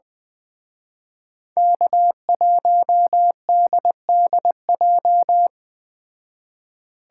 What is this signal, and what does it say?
Morse code, heard as a keyed tone, about 20 wpm, 700 Hz. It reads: K1DDJ
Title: Speed Racing - USA 1x3 Call Signs 17wpm